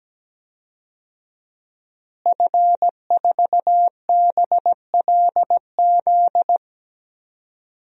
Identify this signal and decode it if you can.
Morse code, heard as a keyed tone, about 17 wpm, 700 Hz. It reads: F4BLZ
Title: DXCC-100 Call Signs - 5 Characters - Repeated 3x - 17wpm